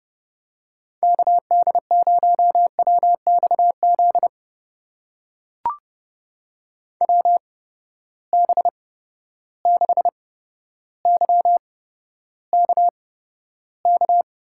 Morse code, heard as a keyed tone, about 30 wpm, spaced as for 13 wpm, 700 Hz. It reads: KD0WXZ WB6YKK
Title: USA Call Signs - 6 Characters - 8x Character Spacing - 30wpm